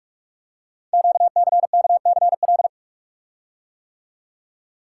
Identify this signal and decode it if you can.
Morse code, heard as a keyed tone, about 44 wpm, 700 Hz. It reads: QCKCL
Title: Code Groups - 5 Letters - 45wpm